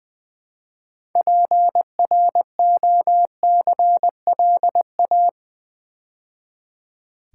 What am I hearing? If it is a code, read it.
Morse code, heard as a keyed tone, about 20 wpm, 700 Hz. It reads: PROCLA
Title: Build A Word - 3 Letters At A Time - 8-Letter Words - 20wpm